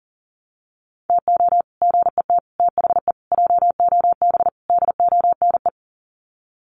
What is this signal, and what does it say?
Morse code, heard as a keyed tone, about 40 wpm, 700 Hz. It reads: TO GET THE JOB DONE
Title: Sets of 5 Words Encore2 40wpm